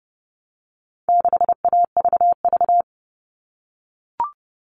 Morse code, heard as a keyed tone, about 30 wpm, 700 Hz. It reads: 6AVV
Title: All Cut Numbers - 4-Digit 30wpm